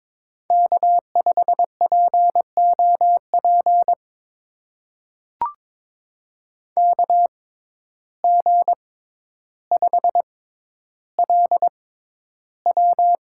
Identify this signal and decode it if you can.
Morse code, heard as a keyed tone, about 21 wpm, spaced as for 11 wpm, 700 Hz. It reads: K5POP KG5LW
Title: USA Call Signs - 6x Character Spacing - 22wpm